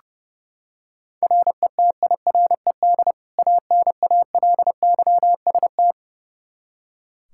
Morse code, heard as a keyed tone, about 30 wpm, 700 Hz. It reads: RETIRED ANALYST
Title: Practice QSO - Exchange 3 - More Info - 30wpm